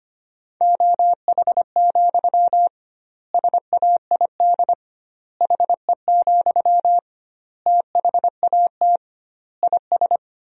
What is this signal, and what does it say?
Morse code, heard as a keyed tone, about 25 wpm, 700 Hz. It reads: OH, SAID HE, THAT IS
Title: 2x Word-Spacing - Sentences from Top 500 Words - 25wpm